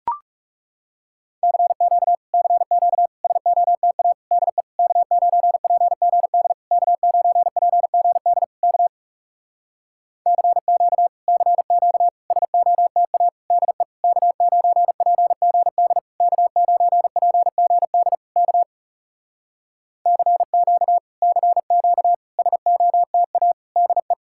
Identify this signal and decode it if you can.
Morse code, heard as a keyed tone, about 40 wpm, 700 Hz. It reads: CQ CQ SOTA DE K9PGD K9PGD K CQ CQ SOTA DE K9PGD K9PGD K CQ CQ SOTA DE